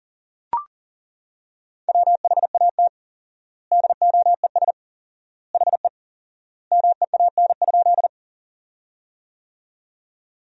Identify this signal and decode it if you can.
Morse code, heard as a keyed tone, about 37 wpm, 700 Hz. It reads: WHAT DOES HE MEAN?
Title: 4x Word-Spacing - Sentences from Top 200 Words - 40wpm